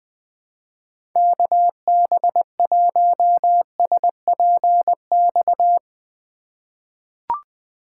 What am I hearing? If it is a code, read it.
Morse code, heard as a keyed tone, about 20 wpm, 700 Hz. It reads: KB1SPX
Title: USA Call Signs - 2x Character Spacing - 20wpm